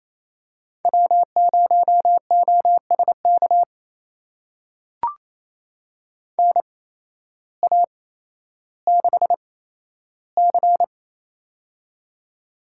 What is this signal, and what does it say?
Morse code, heard as a keyed tone, about 28 wpm, spaced as for 13 wpm, 700 Hz. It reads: W0OSK NA6C
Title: USA Call Signs - 8x Character Spacing - 28wpm